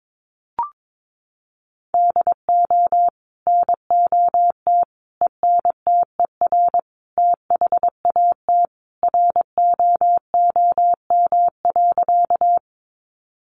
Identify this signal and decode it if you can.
Morse code, heard as a keyed tone, about 22 wpm, 700 Hz. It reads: DO NOT ENTER THAT ROOM.
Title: Sentences from Top 1300 Words 22wpm